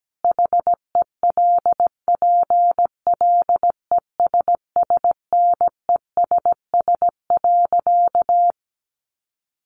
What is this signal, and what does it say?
Morse code, heard as a keyed tone, about 17 wpm, 700 Hz. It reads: HELPLESSNESS.
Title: A Sound Of Thunder 17wpm